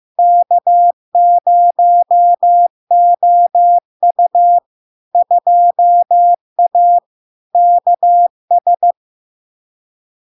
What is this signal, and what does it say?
Morse code, heard as a keyed tone, about 15 wpm, 700 Hz. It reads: K0OU 2A KS
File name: LARK-FD-2022-Log-in-CW-15wpm-700-Hz